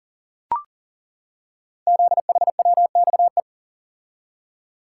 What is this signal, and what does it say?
Morse code, heard as a keyed tone, about 38 wpm, 700 Hz. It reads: ZHWXE
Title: Code Groups - 5 Letters - 40wpm